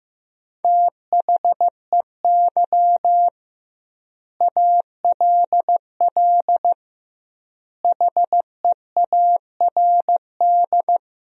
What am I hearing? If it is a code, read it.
Morse code, heard as a keyed tone, about 15 wpm, 700 Hz. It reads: THEY ALL HEARD
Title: 2x Word-Spacing - Sentences from Top 400 Words - 15wpm